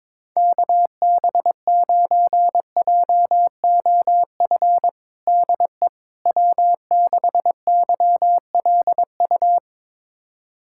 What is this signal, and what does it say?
Morse code, heard as a keyed tone, about 22 wpm, 700 Hz. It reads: KB9JOF DE W6YLU
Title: USA Callsign DE Callsign 22wpm